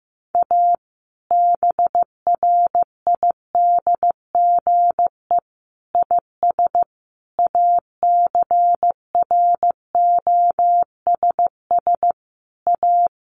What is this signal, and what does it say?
Morse code, heard as a keyed tone, about 15 wpm, 700 Hz. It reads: A BRIDGE IS ACROSS A
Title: Sentences from Top 1200 Words 15wpm